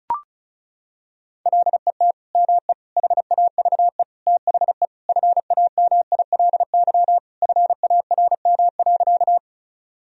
Morse code, heard as a keyed tone, about 34 wpm, 700 Hz. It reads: LET ME HAVE THE FAMILY FARM.
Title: Sentences from Top 1000 Words - Review Part 2 - 35wpm